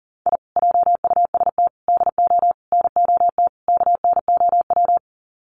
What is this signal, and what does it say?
Morse code, heard as a keyed tone, about 40 wpm, 700 Hz. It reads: I JUST DO NOT KNOW
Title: Sets of 5 Words Encore 40wpm